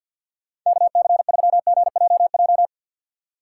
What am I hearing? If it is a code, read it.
Morse code, heard as a keyed tone, about 51 wpm, 700 Hz. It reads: KC3GJJ
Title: USA Call Signs - 6 Characters - 3x Character Spacing - 50wpm